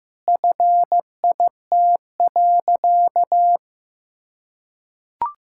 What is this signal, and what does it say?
Morse code, heard as a keyed tone, about 15 wpm, 700 Hz. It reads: FIT.
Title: Sentences from Top 1000 Words - Review Part 5 - 15wpm